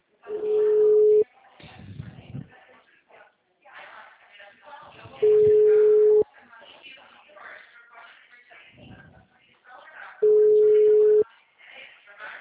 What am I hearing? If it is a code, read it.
Ringback tone (Europe, 425 Hz), with speech.